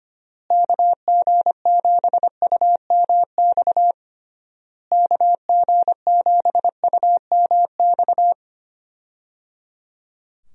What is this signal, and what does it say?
Morse code, heard as a keyed tone, about 25 wpm, 700 Hz. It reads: KG7UMX KG7UMX
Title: USA Call Signs - 6 Characters - Repeated 3x - 25wpm